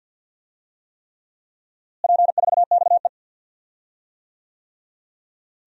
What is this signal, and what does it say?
Morse code, heard as a keyed tone, about 52 wpm, 700 Hz. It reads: P4XE
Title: DXCC-100 Call Signs - 4 Characters - Repeated 3x - 50wpm